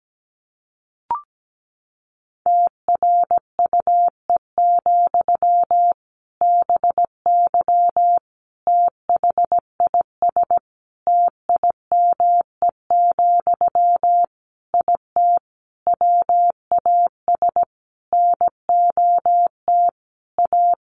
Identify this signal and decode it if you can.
Morse code, heard as a keyed tone, about 17 wpm, 700 Hz. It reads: TRUE, BY THIS TIME, IT WAS NOT A
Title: Sentences from Top 1500 Words 17wpm